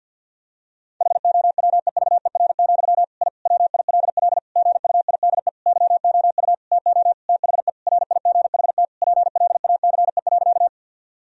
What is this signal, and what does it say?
Morse code, heard as a keyed tone, about 50 wpm, 700 Hz. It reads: HOWEVER, I WILL GUIDE YOU TO THE RIGHT PLACE.
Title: Sentences from Top 1000 Words - Review Part 3 - 50wpm